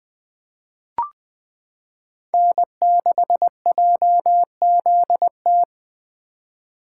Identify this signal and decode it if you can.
Morse code, heard as a keyed tone, about 20 wpm, 700 Hz. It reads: N6JZT
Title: 2023 SOTA Chaser Call Signs - 20wpm